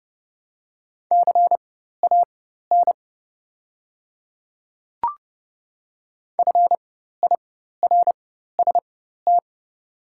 Morse code, heard as a keyed tone, about 30 wpm, spaced as for 14 wpm, 700 Hz. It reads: CAN FIRST
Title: 4x Character Spacing - Top 100 Words - 30wpm